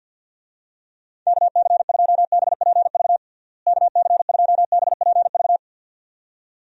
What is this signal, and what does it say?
Morse code, heard as a keyed tone, about 50 wpm, 700 Hz. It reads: KC2BPV KC2BPV
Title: USA Call Signs - 6 Characters - Repeated 2x - 50wpm